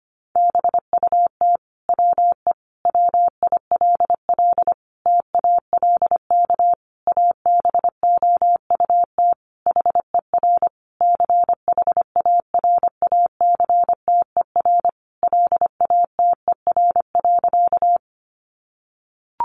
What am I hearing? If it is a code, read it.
Morse code, heard as a keyed tone, about 25 wpm, 700 Hz. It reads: BUT WE WILL TALK ABOUT HER CHARACTER LATER.
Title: Sentences from Top 1200 Words 25wpm